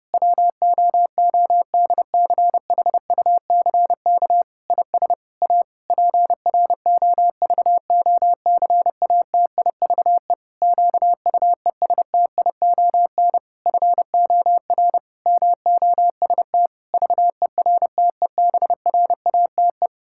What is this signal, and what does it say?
Morse code, heard as a keyed tone, about 30 wpm, 700 Hz. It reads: WOODCHUCK IS A PROVOCATIVE QUESTION FOR MOST VERTEBRATE
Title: Pangram Challenge v2 - 30wpm